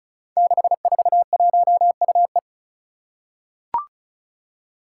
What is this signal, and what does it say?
Morse code, heard as a keyed tone, about 36 wpm, 700 Hz. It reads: 641UE
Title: Five Challenging Letter-Number Combinations 35wpm